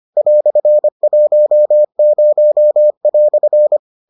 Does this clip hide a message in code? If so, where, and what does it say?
Yes: Morse code from 0.2 to 3.8 s, heard as a keyed tone, about 25 wpm, 600 Hz. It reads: "10"